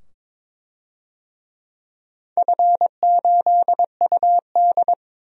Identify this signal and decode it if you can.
Morse code, heard as a keyed tone, about 23 wpm, 700 Hz. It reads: F8UD
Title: DXCC-100 Call Signs - 4 Characters - 22wpm